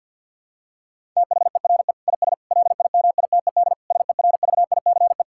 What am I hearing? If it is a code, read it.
Morse code, heard as a keyed tone, about 50 wpm, 700 Hz. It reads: THERE IS LIMITED SERVICE
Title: Sentences from Top 1000 Words - Review Part 6 - 50wpm